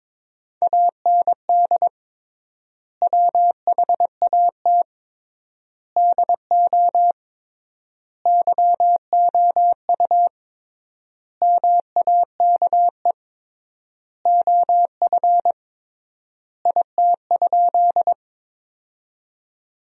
Morse code, heard as a keyed tone, about 22 wpm, 700 Hz. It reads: AND WHAT DO YOU MAKE OF IT?
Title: 3x Word-Spacing - Sentences from Top 100 Words - 22wpm